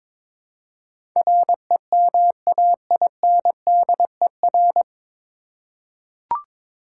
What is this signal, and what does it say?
Morse code, heard as a keyed tone, about 22 wpm, 700 Hz. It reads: REMAINDER
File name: 9 Letter Words - First 3 Letters Given - 22wpm